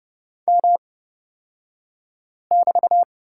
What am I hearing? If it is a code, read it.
Morse code, heard as a keyed tone, about 30 wpm, spaced as for 5 wpm, 700 Hz. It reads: M<BT>
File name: Lesson 36d - BT - Two Characters 30wpm